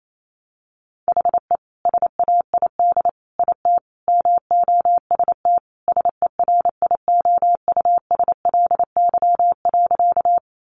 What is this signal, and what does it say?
Morse code, heard as a keyed tone, about 28 wpm, 700 Hz. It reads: HE SAID IT MOST SERIOUSLY.
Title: Sentences from Top 1200 Words 28wpm